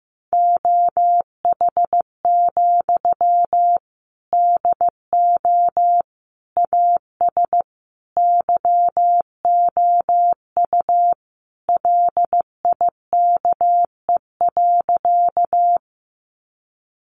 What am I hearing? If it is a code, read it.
Morse code, heard as a keyed tone, about 15 wpm, 700 Hz. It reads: OH, DO AS YOU LIKE.
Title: Sentences from Top 500 Words 15wpm